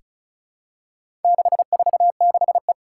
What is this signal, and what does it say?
Morse code, heard as a keyed tone, about 36 wpm, 700 Hz. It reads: 646E
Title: Four Challenging Letter-Number Combinations 35wpm